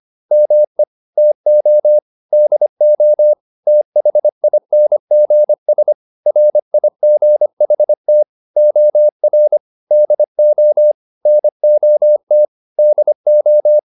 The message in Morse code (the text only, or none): ME TO DO THINGS RIGHT OR DO NOT DO